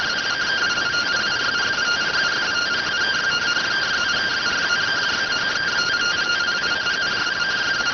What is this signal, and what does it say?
Two-tone teleprinter signal, about 170 Hz shift.